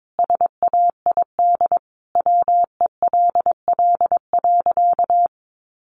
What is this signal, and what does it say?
Morse code, heard as a keyed tone, about 22 wpm, 700 Hz. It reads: SAID WELL.
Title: Sentences from Top 1100 Words 22wpm